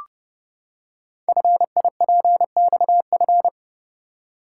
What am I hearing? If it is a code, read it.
Morse code, heard as a keyed tone, about 30 wpm, 700 Hz. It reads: FIPXF
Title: Code Groups - 5 Letters - 30wpm